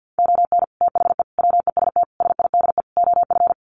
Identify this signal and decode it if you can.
Morse code, heard as a keyed tone, about 48 wpm, 700 Hz. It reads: ON THE WEST SIDE OF
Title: Sets of 5 Words Encore2 50wpm